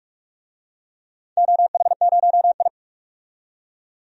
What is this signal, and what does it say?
Morse code, heard as a keyed tone, about 44 wpm, 700 Hz. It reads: OH0I
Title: DXCC-100 Call Signs - 4 Characters - 45wpm